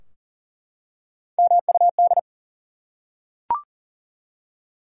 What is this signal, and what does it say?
Morse code, heard as a keyed tone, about 40 wpm, 700 Hz. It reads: MUD
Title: Three Letter Words - 40wpm